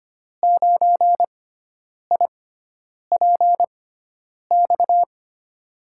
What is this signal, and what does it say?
Morse code, heard as a keyed tone, about 25 wpm, spaced as for 9 wpm, 700 Hz. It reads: 9IPX
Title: USA Call Signs - 6x Character Spacing - 25wpm